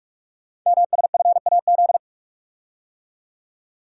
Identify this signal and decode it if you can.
Morse code, heard as a keyed tone, about 44 wpm, 700 Hz. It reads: MSFAZ
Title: Code Groups - 5 Letters - 45wpm